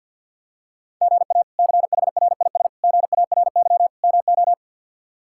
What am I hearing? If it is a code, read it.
Morse code, heard as a keyed tone, about 51 wpm, 700 Hz. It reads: GA CHRIS GARY MO
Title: K1USN SST QSO Practice - 50wpm